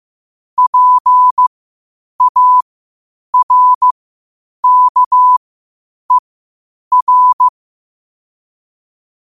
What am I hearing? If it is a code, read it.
Morse code, heard as a keyed tone, about 15 wpm, spaced as for 8 wpm, 1000 Hz. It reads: PARKER
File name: qotdfiles / 8wpm